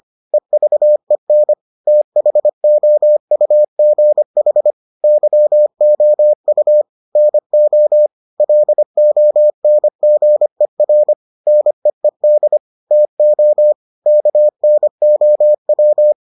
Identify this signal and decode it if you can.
Morse code, heard as a keyed tone, about 25 wpm, 600 Hz. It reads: EVEN THOUGH YOU NO LONGER NEED TO KNOW